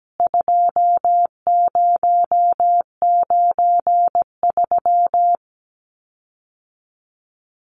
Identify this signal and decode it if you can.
Morse code, heard as a keyed tone, about 17 wpm, 700 Hz. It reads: 2093
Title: CWT Exchange 17wpm